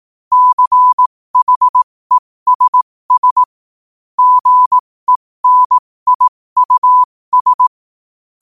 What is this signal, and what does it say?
Morse code, heard as a keyed tone, about 18 wpm, spaced as for 15 wpm, 1000 Hz. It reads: CHESS GENIUS
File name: qotdfiles / 15wpm